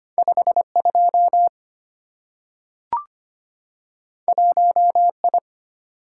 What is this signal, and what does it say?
Morse code, heard as a keyed tone, about 25 wpm, 700 Hz. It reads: 52 1I
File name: Two Letter-Number Combinations - 25wpm